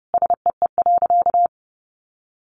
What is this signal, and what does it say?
Morse code, heard as a keyed tone, about 30 wpm, 700 Hz. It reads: SEE.
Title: Sentences from Top 100 Words 30wpm